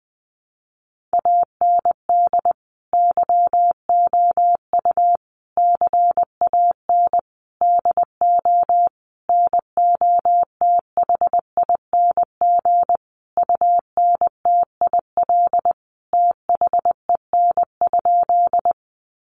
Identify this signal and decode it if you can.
Morse code, heard as a keyed tone, about 20 wpm, 700 Hz. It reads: AND YOU CAN DO NOTHING UNTIL THEN?
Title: Sentences from Top 500 Words 20wpm